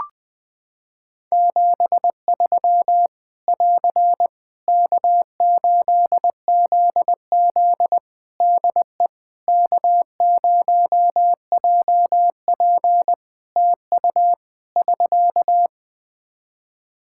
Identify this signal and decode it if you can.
Morse code, heard as a keyed tone, about 20 wpm, 700 Hz. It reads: 73 <AR> K8ZZ DE K0JP TU <SK>
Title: Practice QSO - Exchange 4 - Ending - 20wpm